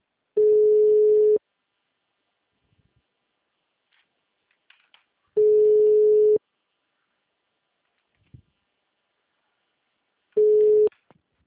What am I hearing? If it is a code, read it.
Ringback tone (Europe, 425 Hz).